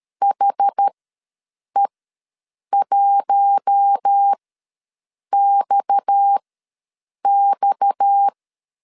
Morse code, heard as a keyed tone, about 13 wpm, spaced as for 7 wpm, 800 Hz. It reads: HE1XX
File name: LEZIONI / 6 PKX0 / 6 PKX ws 12wpm 3106